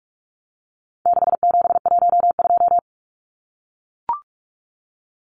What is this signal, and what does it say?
Morse code, heard as a keyed tone, about 44 wpm, 700 Hz. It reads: 6712
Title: Numbers - 4-Digit 45wpm